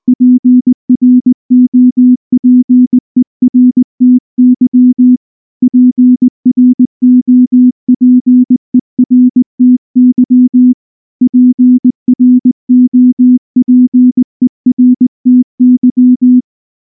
Morse code, heard as a keyed tone, about 19 wpm, 250 Hz. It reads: PROPERTY PROPERTY PROPERTY
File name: morse / morse-training-20WPM-3